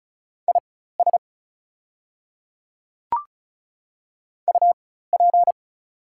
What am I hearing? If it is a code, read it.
Morse code, heard as a keyed tone, about 36 wpm, spaced as for 16 wpm, 700 Hz. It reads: IS UP
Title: 4x Character Spacing - Top 100 Words - 35wpm